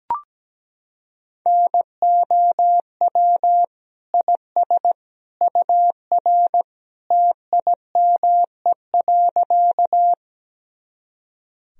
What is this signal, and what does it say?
Morse code, heard as a keyed tone, about 17 wpm, 700 Hz. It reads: NOW IS UR TIME.
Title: Sentences from Top 100 Words and Abbreviations 17wpm